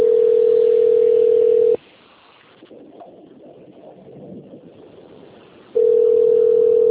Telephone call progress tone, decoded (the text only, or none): ringback tone